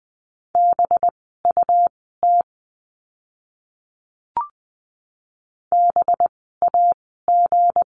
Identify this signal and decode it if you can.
Morse code, heard as a keyed tone, about 20 wpm, spaced as for 14 wpm, 700 Hz. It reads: BUT BAG
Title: 185 Lesson 30f - Letter b - Words 20wpm